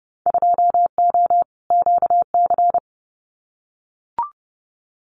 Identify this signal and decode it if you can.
Morse code, heard as a keyed tone, about 30 wpm, 700 Hz. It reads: 2O QC
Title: Winter Field Day Exchanges 30wpm